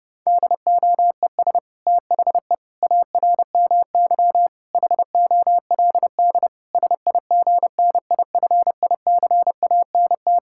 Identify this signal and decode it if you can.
Morse code, heard as a keyed tone, about 32 wpm, 700 Hz. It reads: DOES THE ARMY HOLD SIGNIFICANT
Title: Sentences from Top 1000 Words - Review Part 6 - 30wpm